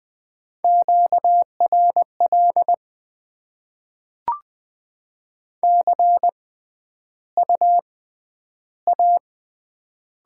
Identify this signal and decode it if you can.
Morse code, heard as a keyed tone, about 20 wpm, spaced as for 12 wpm, 690 Hz. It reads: QRL CU A
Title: Top 20 QSO Elements - 6x Character Spacing - 20wpm